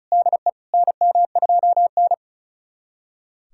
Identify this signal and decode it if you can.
Morse code, heard as a keyed tone, about 36 wpm, 700 Hz. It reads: DE NM2N
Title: USA Callsign DE Callsign - 4 Characters - 35wpm